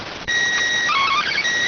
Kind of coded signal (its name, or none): SSTV (slow-scan television)